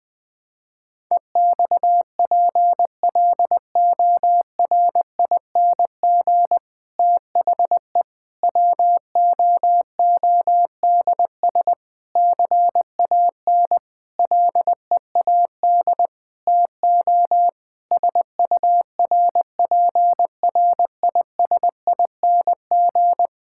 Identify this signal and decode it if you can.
Morse code, heard as a keyed tone, about 20 wpm, 700 Hz. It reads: EXPLORING THE WOODS CAN LEAD TO SURPRISING
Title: Difficult Sentences with Hints - 20wpm